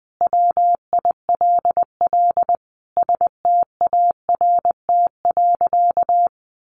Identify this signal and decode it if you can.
Morse code, heard as a keyed tone, about 20 wpm, 700 Hz. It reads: WILL START.
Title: Sentences from Top 300 Words 20wpm